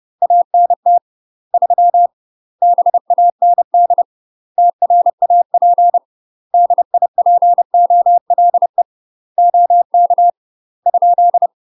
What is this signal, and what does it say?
Morse code, heard as a keyed tone, about 30 wpm, 700 Hz. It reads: ANT 3 BAND TRAP DIPOLE OK ?
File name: UK MASTER QSOs TRAINING / 30WPM / UK MASTER QSOs 0045